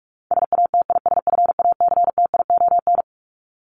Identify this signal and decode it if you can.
Morse code, heard as a keyed tone, about 44 wpm, 700 Hz. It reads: SATISFACTION
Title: Top 1751-2000 Words 45wpm